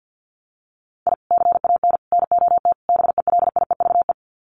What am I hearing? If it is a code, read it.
Morse code, heard as a keyed tone, about 50 wpm, 700 Hz. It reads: I CAN NOT BELIEVE
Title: Sets of 4 Words 50wpm